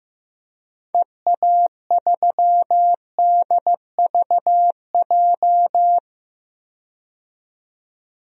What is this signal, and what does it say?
Morse code, heard as a keyed tone, about 15 wpm, 700 Hz. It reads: EA3DVJ
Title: DXCC-100 Call Signs - 6 Characters - 15wpm